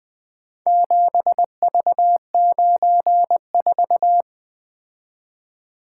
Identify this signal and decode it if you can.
Morse code, heard as a keyed tone, about 20 wpm, 700 Hz. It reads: 7V94
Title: Four Letter-Number Combinations - 20wpm